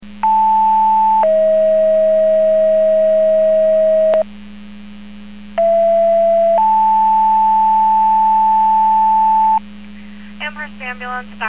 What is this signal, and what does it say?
Two-tone page, heard as a one-second tone, then a three-second tone, about 880 then 640 Hz, about 670 then 880 Hz, with speech. It is heard from 0.2 to 9.6 s.